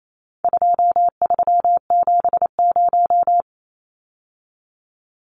Morse code, heard as a keyed tone, about 28 wpm, 700 Hz. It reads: 2370
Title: Numbers - 4-Digit 28wpm